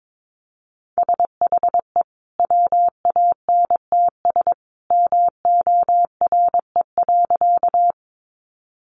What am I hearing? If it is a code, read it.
Morse code, heard as a keyed tone, about 22 wpm, 700 Hz. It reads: SHE WANTS MORE.
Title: Easy Phrases 22wpm